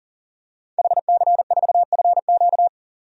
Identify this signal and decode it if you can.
Morse code, heard as a keyed tone, about 40 wpm, 700 Hz. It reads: HC4FQ
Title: DXCC-100 Call Signs - 5 Characters - 40wpm